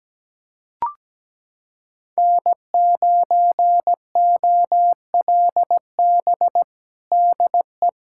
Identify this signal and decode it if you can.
Morse code, heard as a keyed tone, about 17 wpm, 700 Hz. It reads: N9OLB DE
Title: USA Callsign DE Callsign 17wpm